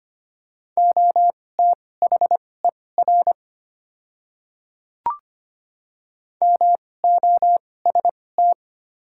Morse code, heard as a keyed tone, about 25 wpm, spaced as for 18 wpm, 700 Hz. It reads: OTHER MOST
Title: 2x Character Spacing - Top 200 Words - 25wpm